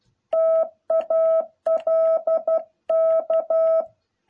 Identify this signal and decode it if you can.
Morse code, heard as a keyed tone, about 12 wpm, 650 Hz. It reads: TALK
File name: jh8cbh / 2021cw